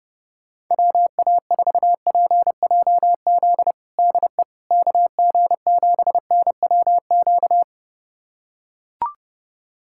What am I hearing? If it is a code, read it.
Morse code, heard as a keyed tone, about 32 wpm, 700 Hz. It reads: WA4PJZ DE KG7NWQ
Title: USA Callsign DE Callsign 30wpm